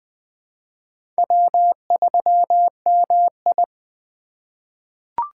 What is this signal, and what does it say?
Morse code, heard as a keyed tone, about 20 wpm, 700 Hz. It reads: W3MI
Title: USA Call Signs - 3x Character Spacing - 20wpm